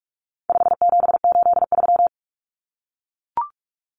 Morse code, heard as a keyed tone, about 44 wpm, 700 Hz. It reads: 5783
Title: Numbers - 4-Digit 45wpm